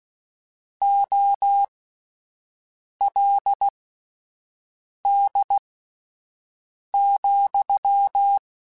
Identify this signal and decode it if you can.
Morse code, heard as a keyed tone, about 16 wpm, spaced as for 5 wpm, 800 Hz. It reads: OLD,
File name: (spaced) / QSOs Spaced-mp3s / WA7QOX 2 @10 sp